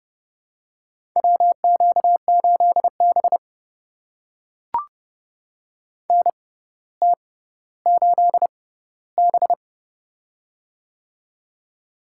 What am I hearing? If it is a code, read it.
Morse code, heard as a keyed tone, about 30 wpm, spaced as for 15 wpm, 700 Hz. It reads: WQ8B NT8B
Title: USA Call Signs - 4 Characters - 6x Character Spacing - 30wpm